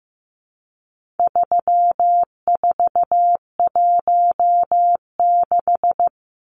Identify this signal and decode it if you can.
Morse code, heard as a keyed tone, about 15 wpm, 700 Hz. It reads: 3416
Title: Numbers - 4-Digit 15wpm